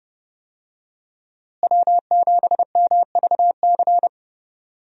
Morse code, heard as a keyed tone, about 32 wpm, 700 Hz. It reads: W7MVC
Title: USA Call Signs - 5 Characters - Repeated 2x - 30wpm